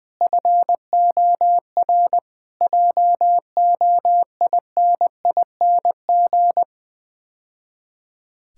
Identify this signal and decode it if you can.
Morse code, heard as a keyed tone, about 20 wpm, 690 Hz. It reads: FOR JOINING
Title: Build A Phrase - 5-Word Phrases - 20wpm